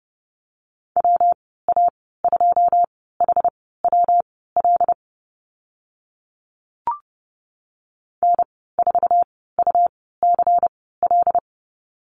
Morse code, heard as a keyed tone, about 31 wpm, spaced as for 17 wpm, 700 Hz. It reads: WA2HWL N4UCL
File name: Lesson 19g - Letter w - Call Signs 30wpm